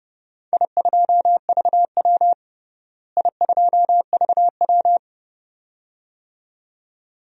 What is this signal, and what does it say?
Morse code, heard as a keyed tone, about 32 wpm, 700 Hz. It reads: I2VW I2VW
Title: DXCC-100 Call Signs - 4 Characters - Repeated 2x - 30wpm